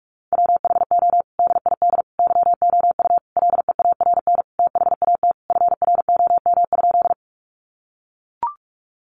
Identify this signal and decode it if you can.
Morse code, heard as a keyed tone, about 44 wpm, 700 Hz. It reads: WHO DID YOU LEARN THAT FROM?
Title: Sentences from Top 300 Words 45wpm